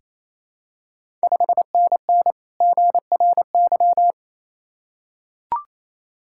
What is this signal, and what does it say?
Morse code, heard as a keyed tone, about 28 wpm, 700 Hz. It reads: 5NN GRY
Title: Ontario QSO Party Exchange 28wpm